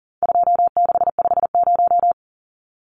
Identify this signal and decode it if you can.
Morse code, heard as a keyed tone, about 40 wpm, 700 Hz. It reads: 2650
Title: Numbers - 4-Digit 40wpm